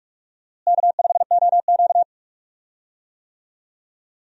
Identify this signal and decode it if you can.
Morse code, heard as a keyed tone, about 46 wpm, 700 Hz. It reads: K5OQ
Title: USA Call Signs - 4 Characters - Repeated 2x - 45wpm